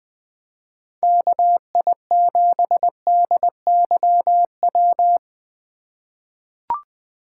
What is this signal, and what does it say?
Morse code, heard as a keyed tone, about 20 wpm, 700 Hz. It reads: KI7DYW
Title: USA Call Signs - 3x Character Spacing - 20wpm